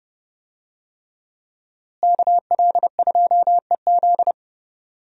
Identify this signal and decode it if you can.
Morse code, heard as a keyed tone, about 30 wpm, 700 Hz. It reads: KL2EZ
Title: USA Call Signs - 5 Characters - Repeated 3x - 30wpm